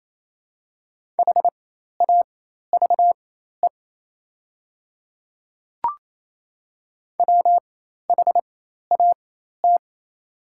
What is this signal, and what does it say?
Morse code, heard as a keyed tone, about 27 wpm, spaced as for 13 wpm, 690 Hz. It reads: HAVE WHAT
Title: 4x Character Spacing - Top 100 Words - 28wpm